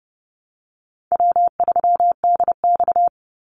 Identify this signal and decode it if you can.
Morse code, heard as a keyed tone, about 30 wpm, 700 Hz. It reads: W3DX
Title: NA SOTA Activator Callsigns 30wpm